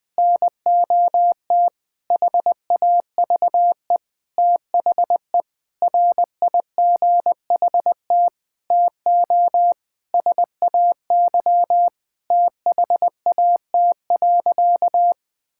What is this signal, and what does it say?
Morse code, heard as a keyed tone, about 21 wpm, 700 Hz. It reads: NOT HAVE THE RIGHT TO SAY THAT.
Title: Sentences with 5-Word Phrases - Spoken Beginning - 20wpm